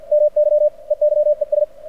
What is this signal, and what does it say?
Morse code, heard as a keyed tone, about 49 wpm, 600 Hz. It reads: MY EYES
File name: archived / CRSnet